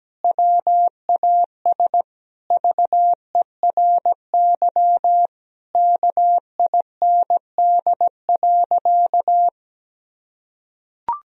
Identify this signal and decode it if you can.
Morse code, heard as a keyed tone, about 17 wpm, 700 Hz. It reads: WAS VERY KIND.
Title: Sentences from Top 1000 Words - Review Part 3 - 17wpm